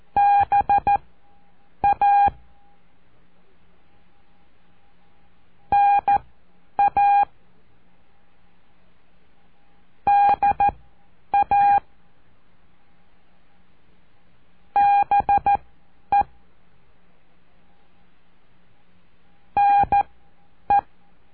Morse code, heard as a keyed tone, about 14 wpm, spaced as for 9 wpm, 810 Hz. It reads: BA NA DA BE NE
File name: cursos / CURSO DE TELEGRAFIA / Telegrafia - Trilha 05